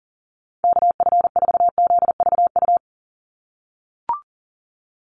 Morse code, heard as a keyed tone, about 42 wpm, 700 Hz. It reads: KF4ZVU
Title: USA Call Signs 40wpm